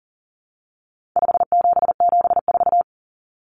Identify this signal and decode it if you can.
Morse code, heard as a keyed tone, about 40 wpm, 700 Hz. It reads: 5774
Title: Common Cut Numbers - 4-Digit 40wpm